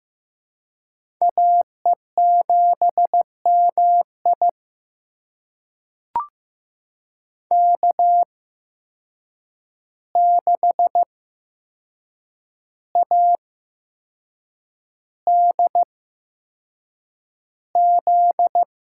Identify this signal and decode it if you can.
Morse code, heard as a keyed tone, about 15 wpm, spaced as for 6 wpm, 700 Hz. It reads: AE7MI K6ADZ
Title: USA Call Signs - 5 Characters - 8x Character Spacing - 15wpm